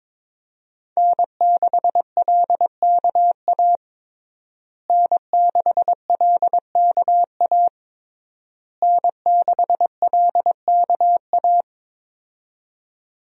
Morse code, heard as a keyed tone, about 22 wpm, 700 Hz. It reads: N6LKA N6LKA N6LKA
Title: USA Call Signs - 5 Characters - Repeated 3x - 22wpm